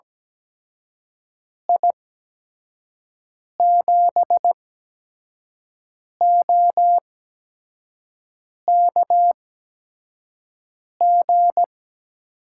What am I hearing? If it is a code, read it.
Morse code, heard as a keyed tone, about 17 wpm, spaced as for 5 wpm, 700 Hz. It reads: I7OKG
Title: USA Call Signs - 6 Characters - 8x Character Spacing - 17wpm